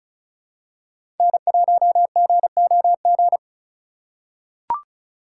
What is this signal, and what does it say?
Morse code, heard as a keyed tone, about 36 wpm, 700 Hz. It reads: N1GOG
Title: USA Call Signs - 5 Characters - 3x Character Spacing - 35wpm